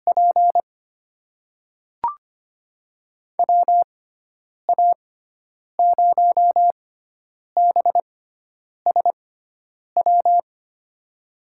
Morse code, heard as a keyed tone, about 25 wpm, spaced as for 9 wpm, 700 Hz. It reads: P WA0BSW
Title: USA Call Signs - 6 Characters - 6x Character Spacing - 25wpm